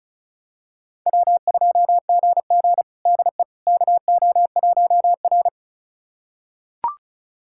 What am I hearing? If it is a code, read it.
Morse code, heard as a keyed tone, about 34 wpm, 700 Hz. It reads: W2GG DE KO1R
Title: USA Callsign DE Callsign - 4 Characters - 35wpm